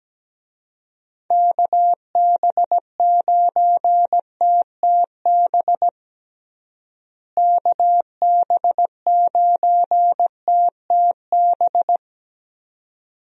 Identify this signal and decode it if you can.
Morse code, heard as a keyed tone, about 17 wpm, 700 Hz. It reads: KB9TTB KB9TTB
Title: USA Call Signs - 6 Characters - Repeated 2x - 17wpm